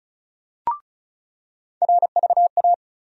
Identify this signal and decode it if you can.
Morse code, heard as a keyed tone, about 36 wpm, 700 Hz. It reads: RVA
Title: USA Call Sign Postfixes - 3 Letters 35wpm